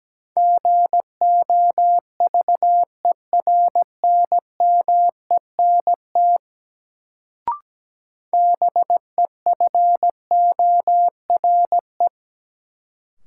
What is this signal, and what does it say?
Morse code, heard as a keyed tone, about 17 wpm, 700 Hz. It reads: GOVERNMENT BEFORE
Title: Sets of 2 Words - New Word Drill - 17wpm